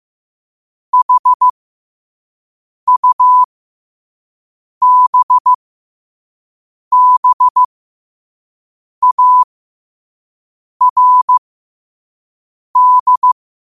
Morse code, heard as a keyed tone, about 15 wpm, spaced as for 5 wpm, 1000 Hz. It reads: HUBBARD
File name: qotdfiles / 5wpm